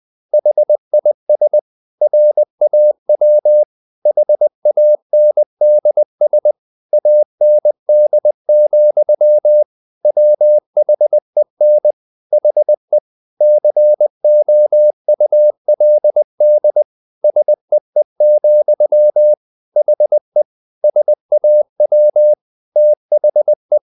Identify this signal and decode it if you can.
Morse code, heard as a keyed tone, about 20 wpm, 600 Hz. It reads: HIS RAW HANDS AND, WHEN HE COULD SEE, HE SAW THE
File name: the old man and the sea part 2 20wpm